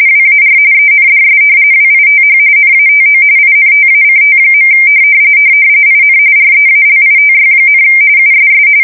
RTTY, about 45.45 baud. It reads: WAR ENDS WITH NATO INTACT. IN REALITY, USS
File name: fsk170 / battleshipnj